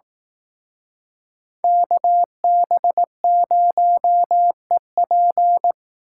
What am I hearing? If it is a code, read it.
Morse code, heard as a keyed tone, about 18 wpm, 700 Hz. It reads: KB0EP
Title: Speed Racing - USA 2x2 Call Signs 15wpm